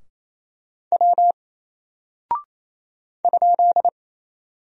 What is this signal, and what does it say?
Morse code, heard as a keyed tone, about 28 wpm, 700 Hz. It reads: W ?
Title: Single Character (Letter:Number:Punctuation:Prosign) Alt 28wpm